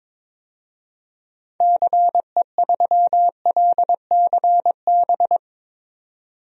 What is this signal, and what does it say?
Morse code, heard as a keyed tone, about 22 wpm, 700 Hz. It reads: CE3LCB
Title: DXCC-100 Call Signs - 6 Characters - 22wpm